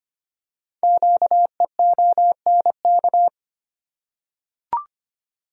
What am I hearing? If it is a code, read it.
Morse code, heard as a keyed tone, about 25 wpm, 700 Hz. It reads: QEONK
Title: Code Groups - 5 Letters - 25wpm